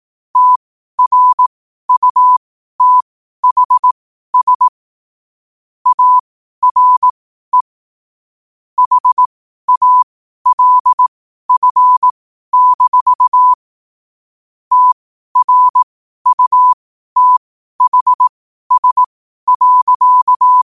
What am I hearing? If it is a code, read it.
Morse code, heard as a keyed tone, about 18 wpm, spaced as for 12 wpm, 1000 Hz. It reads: TRUTHS ARE HALF- TRUTHS.
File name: qotdfiles / 12wpm